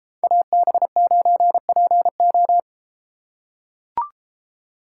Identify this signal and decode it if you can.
Morse code, heard as a keyed tone, about 31 wpm, 700 Hz. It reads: AB9PO